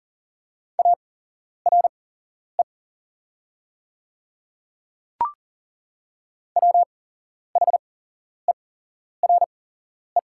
Morse code, heard as a keyed tone, about 40 wpm, spaced as for 11 wpm, 700 Hz. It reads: ARE WHERE
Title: 8x Character Spacing - Top 200 Words - 40wpm